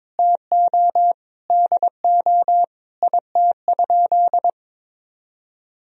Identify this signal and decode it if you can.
Morse code, heard as a keyed tone, about 22 wpm, 700 Hz. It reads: TO DO IT?
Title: Sentences from Top 1000 Words - Review Part 2 - 22wpm